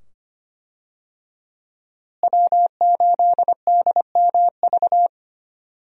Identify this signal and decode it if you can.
Morse code, heard as a keyed tone, about 25 wpm, 700 Hz. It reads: W8DMV
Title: USA Call Signs - 5 Characters - Repeated 2x - 25wpm